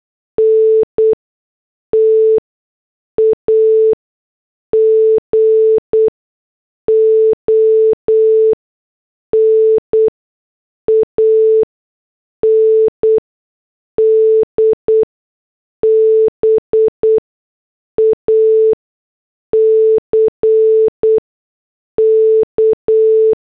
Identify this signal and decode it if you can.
Morse code, heard as a keyed tone, about 8 wpm, spaced as for 6 wpm, 440 Hz. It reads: NTAGON AND BACK
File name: morse / headlines